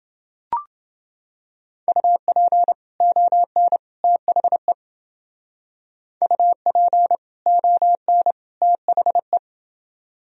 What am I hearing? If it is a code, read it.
Morse code, heard as a keyed tone, about 28 wpm, 700 Hz. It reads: UP ON THE UP ON THE